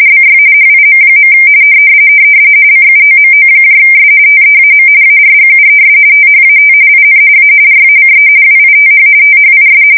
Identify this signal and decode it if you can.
RTTY, about 45.45 baud. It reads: BY HIM. CUOMOS TEAM ARGUES THAT IN A RACE WITH THIS
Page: 170 Shift 2125/2295